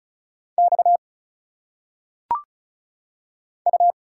Morse code, heard as a keyed tone, about 34 wpm, 700 Hz. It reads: X U
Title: Single Letters - Phonetic Pronunciation 35wpm